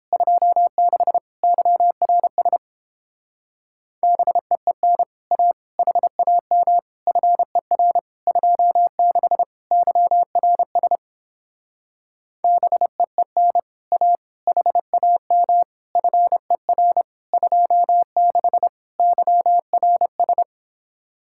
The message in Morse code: 26 YRS BEEN A HAM FER 26 YRS BEEN A HAM FER 26 YRS

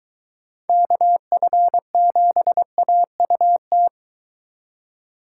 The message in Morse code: KF7AUT

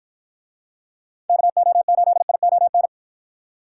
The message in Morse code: KO8ION

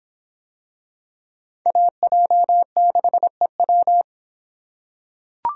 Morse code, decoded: AJ6EW

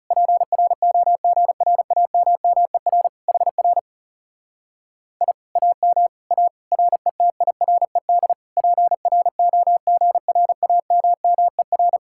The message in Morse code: PROGRAMMER HR I AM A RETIRED PROGRAMMER

